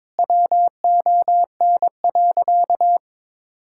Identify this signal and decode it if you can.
Morse code, heard as a keyed tone, about 22 wpm, 700 Hz. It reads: WON.
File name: practice / 22wpm / Sentences with 3-Word Phrases - 22wpm